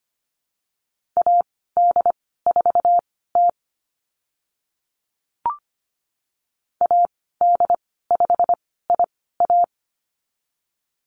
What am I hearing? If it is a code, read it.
Morse code, heard as a keyed tone, about 25 wpm, spaced as for 16 wpm, 700 Hz. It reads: AD4T AD5IA
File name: Lesson 15g - Number 5 - Call Signs 25wpm